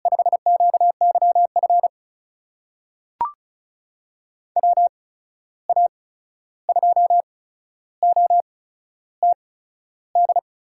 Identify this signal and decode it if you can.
Morse code, heard as a keyed tone, about 36 wpm, spaced as for 13 wpm, 700 Hz. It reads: 5QYF WA2OTD
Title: USA Call Signs - 6 Characters - 8x Character Spacing - 35wpm